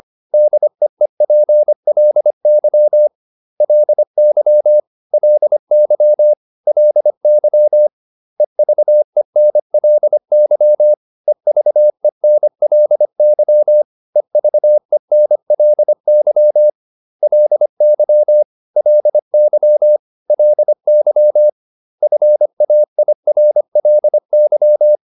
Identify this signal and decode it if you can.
Morse code, heard as a keyed tone, about 25 wpm, 600 Hz. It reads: DEEPLY LY LY LY EVENLY EVENLY EVENLY LY LY LY FAIRLY